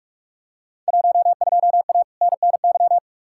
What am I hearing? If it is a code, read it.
Morse code, heard as a keyed tone, about 46 wpm, 700 Hz. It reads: 12A NNY
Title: Field Day Exchanges Encore - 45wpm